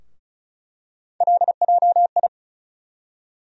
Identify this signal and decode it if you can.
Morse code, heard as a keyed tone, about 36 wpm, 700 Hz. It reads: LJI
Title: USA Call Sign Postfixes - 3 Letters 35wpm